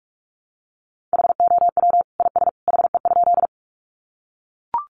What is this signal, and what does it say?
Morse code, heard as a keyed tone, about 44 wpm, 700 Hz. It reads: HOW IS HE?
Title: Sentences from Top 100 Words 45wpm